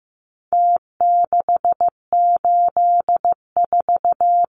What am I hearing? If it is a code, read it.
Morse code, heard as a keyed tone, about 15 wpm, 700 Hz. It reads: T684
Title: Numbers - Contest Style - 4-Digit 15wpm